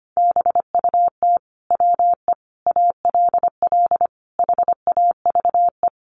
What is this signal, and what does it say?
Morse code, heard as a keyed tone, about 25 wpm, 700 Hz. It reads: BUT WE ALL HAVE
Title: Sentences from Top 1300 Words 25wpm